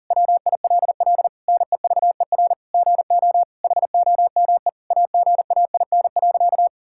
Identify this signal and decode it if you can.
Morse code, heard as a keyed tone, about 40 wpm, 700 Hz. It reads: WILL NEVER GO HOME AGAIN.
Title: Sentences from Top 1000 Words - Review Part 6 - 40wpm